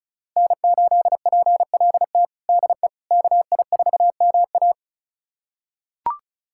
Morse code, heard as a keyed tone, about 36 wpm, 700 Hz. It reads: N8PLT DE KI4MA
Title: USA Callsign DE Callsign - 5 Characters - 35wpm